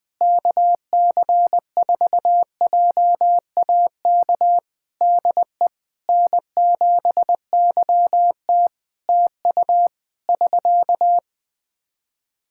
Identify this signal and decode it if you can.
Morse code, heard as a keyed tone, about 20 wpm, 700 Hz. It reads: KC4JAK DE N7YT TU <SK>
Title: Practice QSO - Exchange 4 - Ending - 20wpm